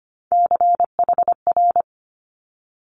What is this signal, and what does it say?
Morse code, heard as a keyed tone, about 25 wpm, 700 Hz. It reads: CHR
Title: Multi-letter Phonemes Quick Fire 25wpm